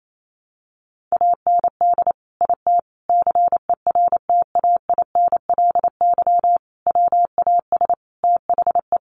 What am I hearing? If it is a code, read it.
Morse code, heard as a keyed tone, about 28 wpm, 700 Hz. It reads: AND IT CERTAINLY WAS THE
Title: Sentences from Top 1100 Words 28wpm